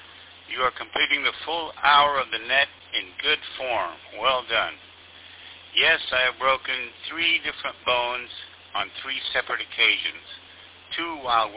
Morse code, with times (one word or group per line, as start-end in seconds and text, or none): none